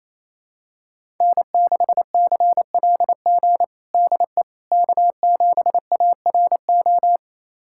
Morse code, heard as a keyed tone, about 28 wpm, 700 Hz. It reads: N6CLG DE K7ARO
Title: USA Callsign DE Callsign - 5 Characters - 28wpm